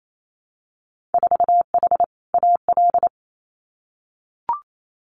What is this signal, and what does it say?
Morse code, heard as a keyed tone, about 28 wpm, 700 Hz. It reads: 4H AL
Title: Winter Field Day Exchanges Encore 28wpm